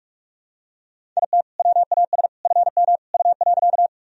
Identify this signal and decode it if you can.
Morse code, heard as a keyed tone, about 44 wpm, 700 Hz. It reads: IT WAS FM U.
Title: Sentences from Top 100 Words and Abbreviations 45wpm